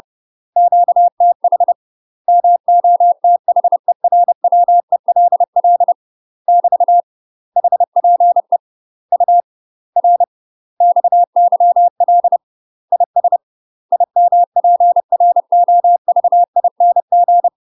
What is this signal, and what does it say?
Morse code, heard as a keyed tone, about 30 wpm, 700 Hz. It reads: QTH MOTHERWELL <BT> HPE U R XYL IS IMPROVING
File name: UK MASTER QSOs TRAINING / 30WPM / UK MASTER QSOs 0014